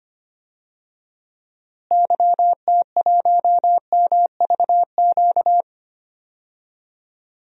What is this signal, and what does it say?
Morse code, heard as a keyed tone, about 25 wpm, 700 Hz. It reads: YT1MVQ
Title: DXCC-100 Call Signs - 6 Characters - Repeated 3x - 25wpm